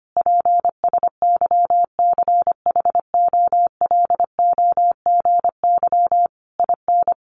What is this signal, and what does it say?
Morse code, heard as a keyed tone, about 25 wpm, 700 Hz. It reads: PSYCHOLOGY IN
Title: Sentences from Top 1400 Words 25wpm